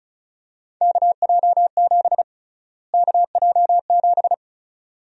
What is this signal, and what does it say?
Morse code, heard as a keyed tone, about 35 wpm, 700 Hz. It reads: KJ7 KJ7
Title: USA Call Sign Prefixes - Repeated 2x - 35wpm